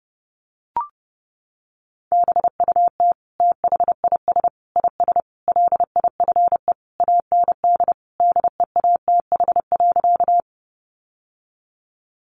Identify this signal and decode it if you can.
Morse code, heard as a keyed tone, about 30 wpm, 700 Hz. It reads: BUT THIS IS LIFE AND DEATH.
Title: Sentences from Top 1100 Words 30wpm